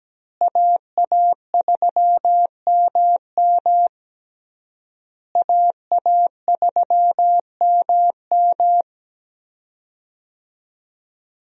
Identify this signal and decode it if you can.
Morse code, heard as a keyed tone, about 17 wpm, 700 Hz. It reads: AA3MM AA3MM
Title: USA Call Signs - 5 Characters - Repeated 3x - 17wpm